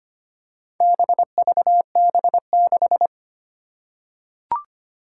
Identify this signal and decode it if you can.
Morse code, heard as a keyed tone, about 25 wpm, 700 Hz. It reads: BVB6
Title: Four Challenging Letter-Number Combinations 25wpm